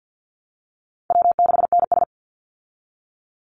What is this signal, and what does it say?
Morse code, heard as a keyed tone, about 49 wpm, 700 Hz. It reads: W6NS
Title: Call Signs Repeated 3x - 50wpm